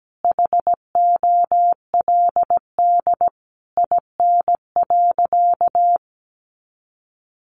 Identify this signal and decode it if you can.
Morse code, heard as a keyed tone, about 17 wpm, 700 Hz. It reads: HOLD IN.
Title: Sentences from Top 400 Words 17wpm